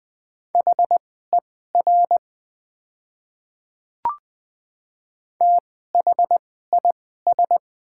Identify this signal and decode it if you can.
Morse code, heard as a keyed tone, about 20 wpm, spaced as for 14 wpm, 700 Hz. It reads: HER THIS
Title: 2x Character Spacing - Top 100 Words - 20wpm